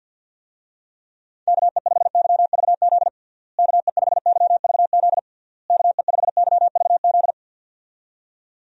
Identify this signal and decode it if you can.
Morse code, heard as a keyed tone, about 51 wpm, 700 Hz. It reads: KE5YVZ KE5YVZ KE5YVZ
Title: USA Call Signs - 6 Characters - Repeated 3x - 50wpm